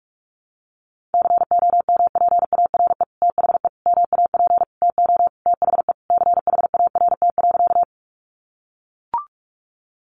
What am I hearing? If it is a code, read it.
Morse code, heard as a keyed tone, about 44 wpm, 700 Hz. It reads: COMPARE THE MAP TO THE CHART.
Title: Sentences from Top 1300 Words 45wpm